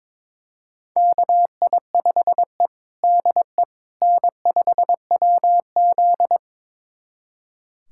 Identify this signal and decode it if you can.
Morse code, heard as a keyed tone, about 22 wpm, 700 Hz. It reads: KI5E DE N5WZ
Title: USA Callsign DE Callsign - 4 Characters - 22wpm